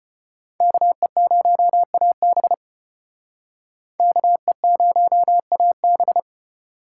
Morse code, heard as a keyed tone, about 31 wpm, 700 Hz. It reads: KE0AB KE0AB
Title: Speed Racing - USA 2x2 Call Signs 25wpm